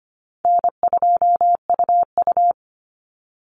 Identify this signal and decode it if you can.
Morse code, heard as a keyed tone, about 25 wpm, 700 Hz. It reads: N2UU
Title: Call Signs - 25wpm